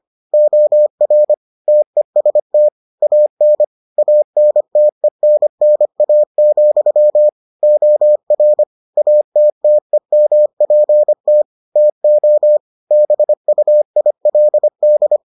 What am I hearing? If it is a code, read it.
Morse code, heard as a keyed tone, about 25 wpm, 600 Hz. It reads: OR TEST AN ANTENNA, OR ATTEMPT TO BUILD